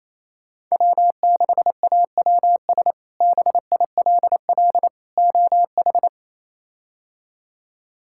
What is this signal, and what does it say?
Morse code, heard as a keyed tone, about 28 wpm, 700 Hz. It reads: W6AWS BILL OH
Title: K1USN SST QSO Practice - 28wpm